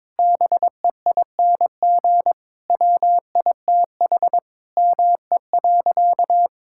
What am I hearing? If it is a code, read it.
Morse code, heard as a keyed tone, about 22 wpm, 700 Hz. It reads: BEING WITH ME.
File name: Sentences with 9-Word Phrases - 22wpm